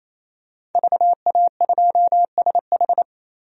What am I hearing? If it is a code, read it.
Morse code, heard as a keyed tone, about 28 wpm, 700 Hz. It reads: VA2SH
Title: DXCC-100 Call Signs - 5 Characters - Repeated 3x - 28wpm